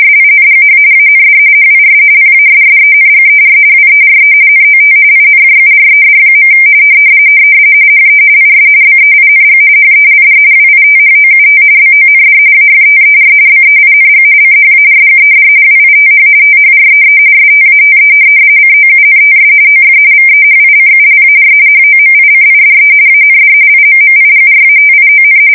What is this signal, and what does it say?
RTTY, about 45.45 baud. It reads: SINGH AND HIS PASSENGER WERE NOT INJURED. HE IS BEING HELD WITHOUT BOND AFTER BEING CHARGED WITH THREE STATE COUNTS OF VEHICULAR HOMICIDE AND IMMIGRATION
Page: 170 Shift 2125/2295